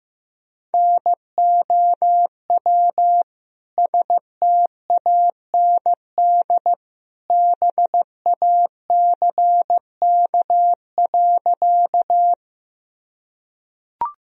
Morse code, heard as a keyed tone, about 15 wpm, 700 Hz. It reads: NOW STAND BACK.
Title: Sentences from Top 1000 Words - Review Part 1 - 15wpm